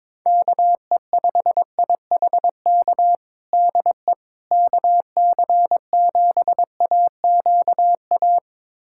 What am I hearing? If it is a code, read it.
Morse code, heard as a keyed tone, about 22 wpm, 700 Hz. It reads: KE5IHK DE KC7AQA
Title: USA Callsign DE Callsign - 6 Characters - 22wpm